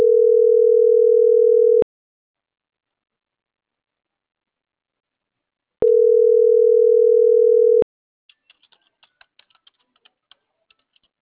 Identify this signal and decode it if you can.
Ringback tone (North America), 440 + 480 Hz.